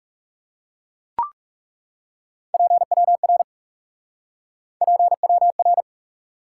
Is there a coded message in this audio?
Yes: Morse code from 2.5 to 5.8 s, heard as a keyed tone, about 39 wpm, 700 Hz. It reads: PWR PWR